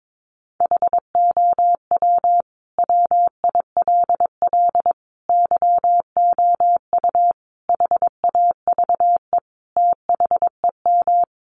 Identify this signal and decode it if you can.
Morse code, heard as a keyed tone, about 22 wpm, 700 Hz. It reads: HOW WILL YOU HAVE THEM
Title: Sentences from Top 900 Words 22wpm